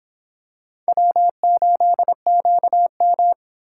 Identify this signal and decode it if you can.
Morse code, heard as a keyed tone, about 26 wpm, 700 Hz. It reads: W8QM